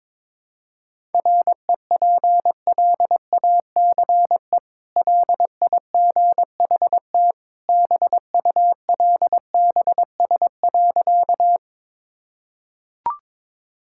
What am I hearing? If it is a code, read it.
Morse code, heard as a keyed tone, about 22 wpm, 700 Hz. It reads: REPLACE LIGHT BULBS.
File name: practice / 22wpm / Easy Sentences with Hints - 22wpm